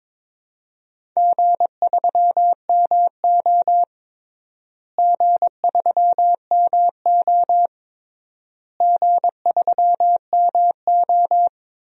Morse code, heard as a keyed tone, about 22 wpm, 700 Hz. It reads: G3MO G3MO G3MO
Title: DXCC-100 Call Signs - 4 Characters - Repeated 3x - 22wpm